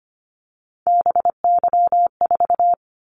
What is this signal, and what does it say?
Morse code, heard as a keyed tone, about 25 wpm, 700 Hz. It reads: BY4
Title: Callsign Prefixes for Top 100 DXCC - 25wpm